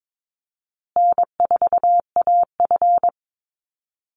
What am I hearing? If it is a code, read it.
Morse code, heard as a keyed tone, about 22 wpm, 700 Hz. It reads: N4AF
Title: Easy Call Signs Repeated 3x - 22wpm